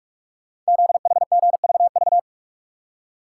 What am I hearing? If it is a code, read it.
Morse code, heard as a keyed tone, about 44 wpm, 690 Hz. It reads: ZHGVV
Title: Code Groups - 5 Letters - 45wpm